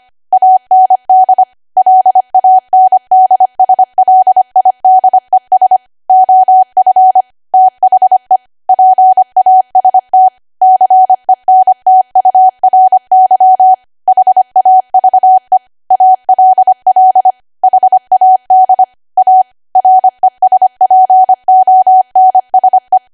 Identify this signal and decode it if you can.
Morse code, heard as a keyed tone, about 25 wpm, 750 Hz. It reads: AND LANDSLIDES OF THE PAST CENTURY HAVE ALL HAD A RESPONSE
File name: Archive / 25 WPM / 25WPM